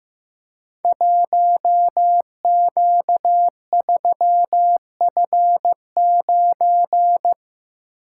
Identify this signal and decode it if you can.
Morse code, heard as a keyed tone, about 15 wpm, 700 Hz. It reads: 1Q3F9
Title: Code Groups - 5 Letter:Numbers - 15wpm